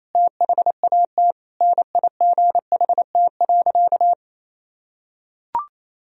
Morse code, heard as a keyed tone, about 28 wpm, 700 Hz. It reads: THAT NIGHT.
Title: Sentences from Top 1000 Words - Review Part 2 - 28wpm